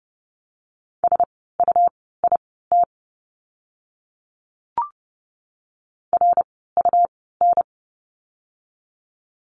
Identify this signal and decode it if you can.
Morse code, heard as a keyed tone, about 30 wpm, spaced as for 17 wpm, 700 Hz. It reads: SUIT RUN
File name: Lesson 17f - Letter u - Words 30wpm